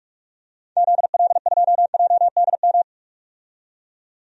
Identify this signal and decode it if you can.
Morse code, heard as a keyed tone, about 43 wpm, 700 Hz. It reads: ZL2JDM
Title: DXCC-100 Call Signs - 6 Characters - 45wpm